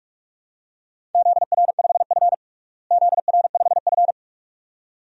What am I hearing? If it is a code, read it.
Morse code, heard as a keyed tone, about 43 wpm, 700 Hz. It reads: ZR5F ZR5F
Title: DXCC-100 Call Signs - 4 Characters - Repeated 2x - 45wpm